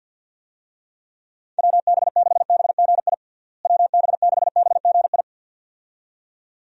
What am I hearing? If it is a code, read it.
Morse code, heard as a keyed tone, about 52 wpm, 700 Hz. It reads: WB6BGI WB6BGI
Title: USA Call Signs - 6 Characters - Repeated 2x - 50wpm